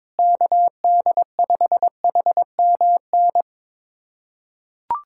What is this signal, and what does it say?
Morse code, heard as a keyed tone, about 22 wpm, 700 Hz. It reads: KD5HMN
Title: USA Callsign DE Callsign - 6 Characters - 22wpm